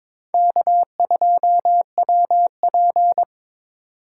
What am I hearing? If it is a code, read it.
Morse code, heard as a keyed tone, about 22 wpm, 700 Hz. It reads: K2WP
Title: USA Call Signs - 4 Characters - 4x Character Spacing - 22wpm